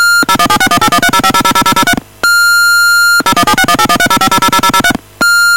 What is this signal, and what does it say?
DTMF, heard as telephone touch tones, tones of about 70 ms. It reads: 0176181305000003 0176181305000003